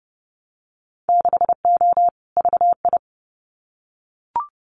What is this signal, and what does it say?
Morse code, heard as a keyed tone, about 30 wpm, 700 Hz. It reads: 6O VI
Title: Winter Field Day Exchanges Encore 30wpm